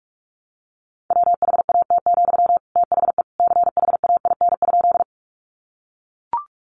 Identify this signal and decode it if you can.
Morse code, heard as a keyed tone, about 44 wpm, 700 Hz. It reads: WHAT, THE CHAIN?
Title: Sentences from Top 1500 Words 45wpm